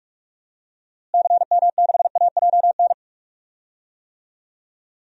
Morse code, heard as a keyed tone, about 46 wpm, 700 Hz. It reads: CM6AJN
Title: DXCC-100 Call Signs - 6 Characters - Repeated 3x - 45wpm